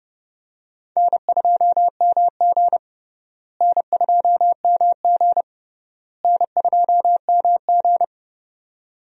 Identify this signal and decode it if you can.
Morse code, heard as a keyed tone, about 30 wpm, 700 Hz. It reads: N2MG N2MG N2MG
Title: USA Call Signs - 4 Characters - Repeated 3x - 30wpm